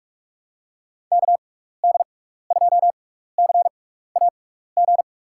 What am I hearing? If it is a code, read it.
Morse code, heard as a keyed tone, about 46 wpm, spaced as for 15 wpm, 700 Hz. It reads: KD2CAG
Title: USA Call Signs - 6x Character Spacing - 45wpm